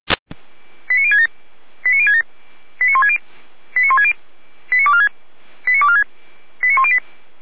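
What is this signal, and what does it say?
Five-tone selective call (ZVEI), heard as quick runs of five tones, tones of about 70 ms. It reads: 89067 89067 89160 89160 89246 89246 89108
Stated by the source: ZVEI five-tone alert received by SDR.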